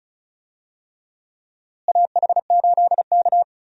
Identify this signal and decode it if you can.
Morse code, heard as a keyed tone, about 36 wpm, 700 Hz. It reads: AH8K
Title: USA Call Signs - 4 Characters - Repeated 3x - 35wpm